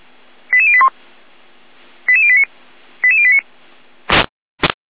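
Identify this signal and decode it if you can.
Five-tone selective call (ZVEI), heard as quick runs of five tones, tones of about 70 ms. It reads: 88081 88089 88089
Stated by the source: ZVEI five-tone call received on SDR.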